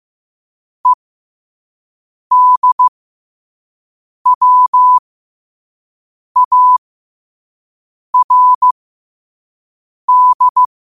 Morse code, heard as a keyed tone, about 15 wpm, spaced as for 5 wpm, 1000 Hz. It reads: EDWARD
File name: qotdfiles / 5wpm